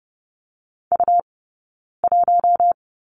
Morse code, heard as a keyed tone, about 30 wpm, 700 Hz. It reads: U 1
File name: Lesson 20d - Number 3 - Two Characters 30wpm